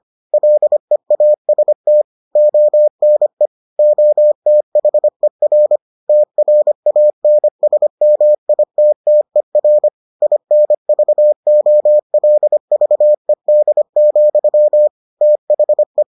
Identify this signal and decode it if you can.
Morse code, heard as a keyed tone, about 25 wpm, 600 Hz. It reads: LEAST ONE OTHER TRANSMITTER INVOLVED, THE